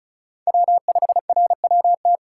Morse code, heard as a keyed tone, about 34 wpm, 700 Hz. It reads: W5RWT
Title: USA Call Signs - 5 Characters - 3x Character Spacing - 35wpm